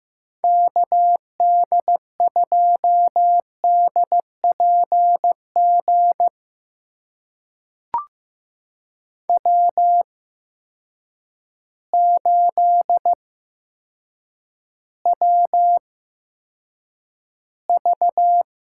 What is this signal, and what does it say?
Morse code, heard as a keyed tone, about 15 wpm, spaced as for 8 wpm, 700 Hz. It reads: KD2DPG W8WV
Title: USA Call Signs - 8x Character Spacing - 15wpm